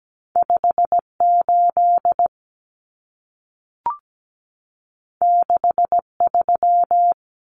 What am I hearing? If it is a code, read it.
Morse code, heard as a keyed tone, about 17 wpm, 700 Hz. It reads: 58 63
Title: Common Cut Numbers - 2-Digit 17wpm